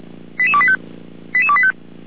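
Five-tone selective call (ZVEI), heard as quick runs of five tones, tones of about 70 ms. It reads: 88285 88285
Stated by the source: ZVEI five-tone selective call 88283 received on an SDR.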